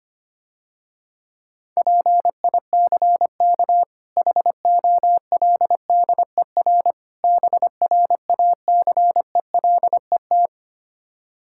Morse code, heard as a keyed tone, about 25 wpm, 700 Hz. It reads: PICK HOLDER BRACELET
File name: practice / 25wpm / Difficult Words - Speed-Racing with Hints - 25wpm